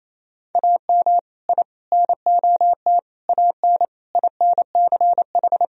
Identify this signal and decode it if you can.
Morse code, heard as a keyed tone, about 27 wpm, 700 Hz. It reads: AM I NOT AN INCH
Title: Sentences from Top 1000 Words - Review Part 3 - 28wpm